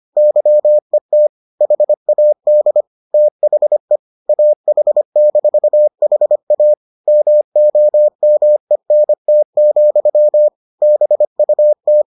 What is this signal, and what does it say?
Morse code, heard as a keyed tone, about 25 wpm, 600 Hz. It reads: YET HAD THE AH-HA MOMENT, BUT